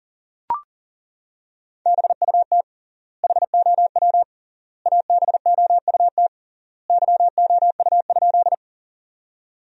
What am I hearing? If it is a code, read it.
Morse code, heard as a keyed tone, about 40 wpm, 700 Hz. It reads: BUT HOW ABOUT YOU?
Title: 3x Word-Spacing - Sentences from Top 100 Words - 40wpm